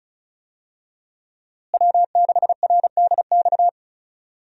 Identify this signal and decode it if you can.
Morse code, heard as a keyed tone, about 36 wpm, 700 Hz. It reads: W6RDX
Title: USA Call Signs - 5 Characters - Repeated 2x - 35wpm